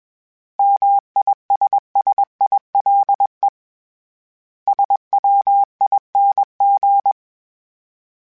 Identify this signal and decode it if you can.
Morse code, heard as a keyed tone, about 21 wpm, 800 Hz. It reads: MISSILE SWING
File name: Misc / Misc-mp3s / Words @20